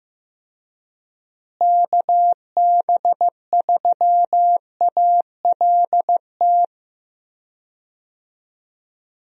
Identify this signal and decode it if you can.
Morse code, heard as a keyed tone, about 15 wpm, 700 Hz. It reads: KB3ALT
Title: USA Call Signs - 6 Characters - Repeated 3x - 15wpm